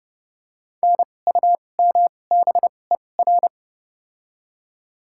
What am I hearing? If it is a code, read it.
Morse code, heard as a keyed tone, about 32 wpm, spaced as for 22 wpm, 700 Hz. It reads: NUMBER
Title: 2x Character Spacing - Top 500 Words - 30wpm